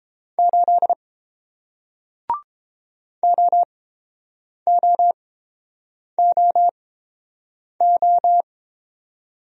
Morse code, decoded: 8 OOOO

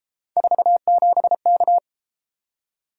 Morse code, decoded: 47K

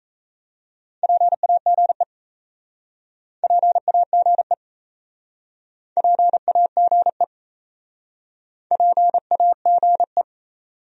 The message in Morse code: PAGE PAGE PAGE PAGE

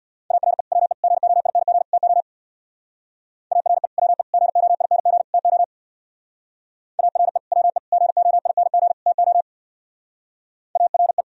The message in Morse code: ARE WE GOING TO ARE WE GOING TO ARE WE GOING TO ARE

